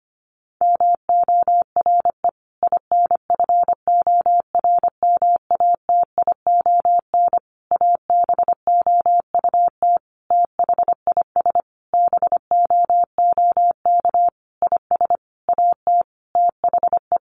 Morse code, heard as a keyed tone, about 25 wpm, 700 Hz. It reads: MORE INFORMATION ABOUT THIS BOOK IS AT THE
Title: Sentences from Top 1200 Words 25wpm